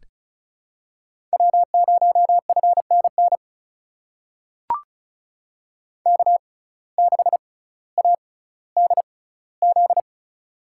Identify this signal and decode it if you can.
Morse code, heard as a keyed tone, about 36 wpm, spaced as for 18 wpm, 700 Hz. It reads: W0FNN K6ADZ
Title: USA Call Signs - 5 Characters - 6x Character Spacing - 35wpm